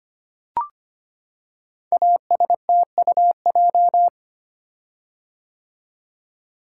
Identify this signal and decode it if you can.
Morse code, heard as a keyed tone, about 25 wpm, 700 Hz. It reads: ASTUJ
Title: Code Groups - 5 Letters - 25wpm